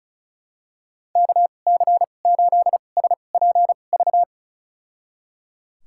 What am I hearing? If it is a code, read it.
Morse code, heard as a keyed tone, about 36 wpm, spaced as for 25 wpm, 700 Hz. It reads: KC8SPV
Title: USA Call Signs - 2x Character Spacing - 35wpm